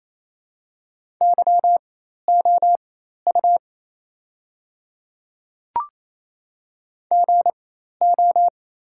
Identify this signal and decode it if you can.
Morse code, heard as a keyed tone, about 28 wpm, spaced as for 13 wpm, 690 Hz. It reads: YOU GO
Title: 4x Character Spacing - Top 100 Words - 28wpm